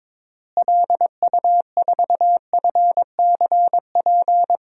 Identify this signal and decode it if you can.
Morse code, heard as a keyed tone, about 22 wpm, 700 Hz. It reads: LU4FCP
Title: DXCC-100 Call Signs - 6 Characters - Repeated 2x - 22wpm